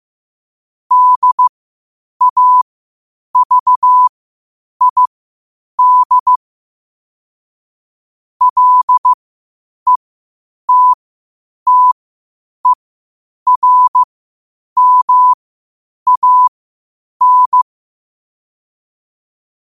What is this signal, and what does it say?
Morse code, heard as a keyed tone, about 15 wpm, spaced as for 8 wpm, 1000 Hz. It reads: DAVID LETTERMAN